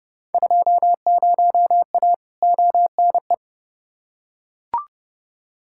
Morse code, heard as a keyed tone, about 31 wpm, 700 Hz. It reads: 20A ONE
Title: Field Day Exchanges Encore - 30wpm